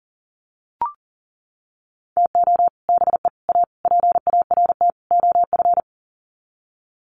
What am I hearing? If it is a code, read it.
Morse code, heard as a keyed tone, about 40 wpm, 700 Hz. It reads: TO BE A PART OF
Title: Sets of 5 Words 40wpm